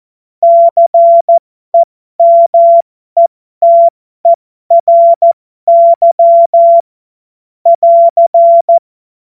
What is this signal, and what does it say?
Morse code, heard as a keyed tone, about 14 wpm, spaced as for 12 wpm, 680 Hz. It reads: CEMETERY <AR>
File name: Morsekurs / Witze